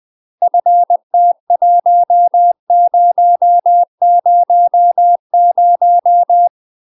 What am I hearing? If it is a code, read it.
Morse code, heard as a keyed tone, about 20 wpm, 700 Hz. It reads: FT1000
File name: UK MASTER QSOs TRAINING / 20WPM / UK MASTER QSOs 0035